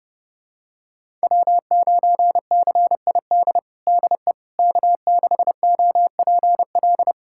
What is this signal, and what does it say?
Morse code, heard as a keyed tone, about 30 wpm, 700 Hz. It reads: W9CID DE K6OPL
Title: USA Callsign DE Callsign - 5 Characters - 30wpm